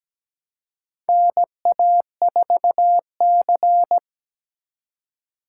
Morse code, heard as a keyed tone, about 17 wpm, 700 Hz. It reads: NA4C
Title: K1USN SST QSO Practice - 17wpm